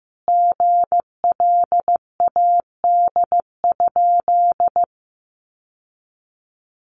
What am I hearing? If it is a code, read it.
Morse code, heard as a keyed tone, about 15 wpm, 700 Hz. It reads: GLAD?
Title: Sentences from Top 800 Words 15wpm